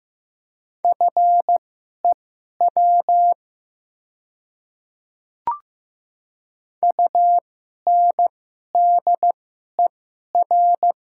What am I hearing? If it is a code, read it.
Morse code, heard as a keyed tone, about 15 wpm, spaced as for 11 wpm, 700 Hz. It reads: FEW UNDER
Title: 2x Character Spacing - Top 500 Words - 15wpm